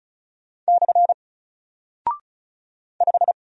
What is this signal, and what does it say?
Morse code, heard as a keyed tone, about 36 wpm, 700 Hz. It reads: / 5
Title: Single Character (Letter:Number:Punctuation:Prosign) Alt 35wpm